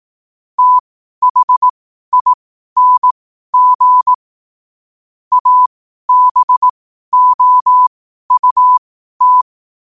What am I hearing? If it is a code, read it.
Morse code, heard as a keyed tone, about 18 wpm, spaced as for 12 wpm, 1000 Hz. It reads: THING ABOUT
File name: qotdfiles / 12wpm